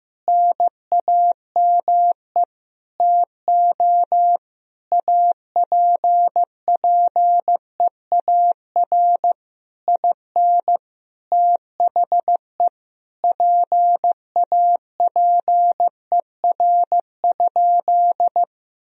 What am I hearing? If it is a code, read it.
Morse code, heard as a keyed tone, about 16 wpm, 700 Hz. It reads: NAME TO APPEAR IN THE PAPER?
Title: Sentences from Top 1000 Words - Review Part 2 - 15wpm